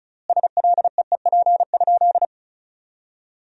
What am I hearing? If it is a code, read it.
Morse code, heard as a keyed tone, about 36 wpm, 700 Hz. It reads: SLEEP?
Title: 4x Word-Spacing - Sentences from Top 500 Words - 35wpm